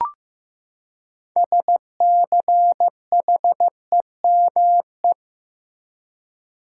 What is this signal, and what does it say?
Morse code, heard as a keyed tone, about 16 wpm, 700 Hz. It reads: SCHEME
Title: Six Letter Words - 15wpm